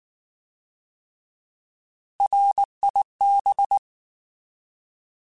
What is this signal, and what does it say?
Morse code, heard as a keyed tone, about 19 wpm, 800 Hz. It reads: RIB
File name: Alphabetical / Alphabetical mp3s / Rs @18